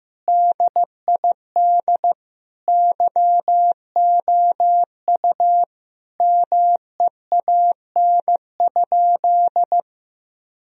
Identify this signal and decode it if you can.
Morse code, heard as a keyed tone, about 14 wpm, 700 Hz. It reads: DID YOU MEAN?
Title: Sentences from Top 1000 Words - Review Part 3 - 15wpm